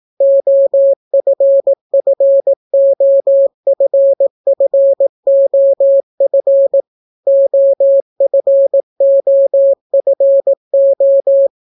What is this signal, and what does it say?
Morse code, heard as a keyed tone, about 18 wpm, 550 Hz. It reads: OFFOFFOF OFOFO